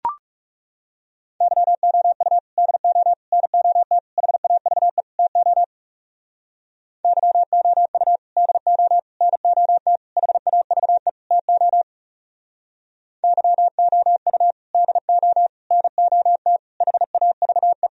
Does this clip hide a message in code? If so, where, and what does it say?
Yes: Morse code from 1.4 to 17.9 s, heard as a keyed tone, about 40 wpm, 700 Hz. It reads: YOU DO NOT HAVE TO YOU DO NOT HAVE TO YOU DO NOT HAVE